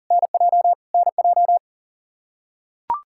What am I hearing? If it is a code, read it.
Morse code, heard as a keyed tone, about 40 wpm, 700 Hz. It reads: NJ NJ
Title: K1USN SST QSO Practice 40wpm